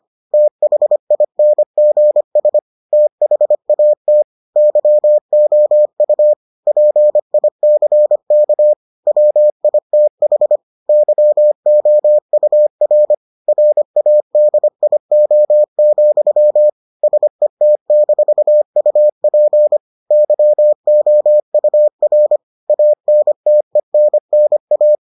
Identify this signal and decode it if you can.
Morse code, heard as a keyed tone, about 25 wpm, 600 Hz. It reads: THINGS THAT YOU PICK WITH YOUR RADIO, SET-UP YOUR ANTENNA